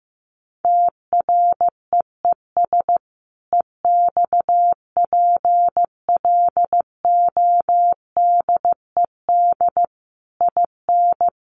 Morse code, heard as a keyed tone, about 15 wpm, 700 Hz. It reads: TREES EXPLODED IN
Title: A Sound Of Thunder 15wpm